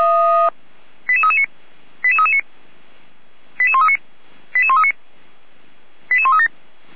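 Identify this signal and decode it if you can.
Five-tone selective call (ZVEI), heard as quick runs of five tones, tones of about 70 ms. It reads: 88339 88339 88139 88139 88137
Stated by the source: Five-tone ZVEI selective call received by SDR, from the Orlamünde fire brigade (feuerwehr-orlamuende) audio archive.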